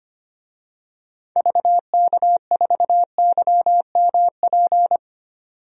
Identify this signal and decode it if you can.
Morse code, heard as a keyed tone, about 25 wpm, 700 Hz. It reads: VK4YMP
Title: DXCC-100 Call Signs - 6 Characters - 25wpm